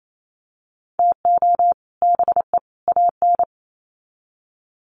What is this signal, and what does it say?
Morse code, heard as a keyed tone, about 28 wpm, 700 Hz. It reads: TO BE AN
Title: Sets of 3 Words 28wpm